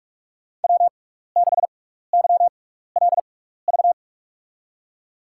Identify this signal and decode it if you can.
Morse code, heard as a keyed tone, about 46 wpm, spaced as for 15 wpm, 690 Hz. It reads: W6YLV
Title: USA Call Signs - 5 Characters - 6x Character Spacing - 45wpm